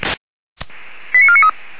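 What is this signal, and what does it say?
Five-tone selective call (ZVEI), heard as quick runs of five tones, tones of about 70 ms. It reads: 89493
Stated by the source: ZVEI five-tone selective call received with an SDR.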